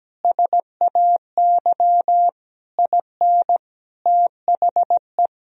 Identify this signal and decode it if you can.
Morse code, heard as a keyed tone, about 17 wpm, 700 Hz. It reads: SAY IN THE
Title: Sentences from Top 1000 Words - Review Part 2 - 17wpm